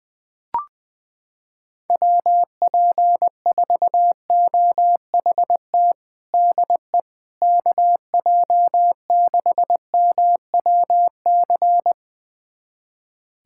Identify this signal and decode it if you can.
Morse code, heard as a keyed tone, about 20 wpm, 700 Hz. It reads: WP4OHT DE KJ6MWC
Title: USA Callsign DE Callsign 20wpm